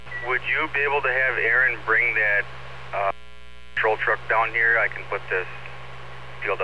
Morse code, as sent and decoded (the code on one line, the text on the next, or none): none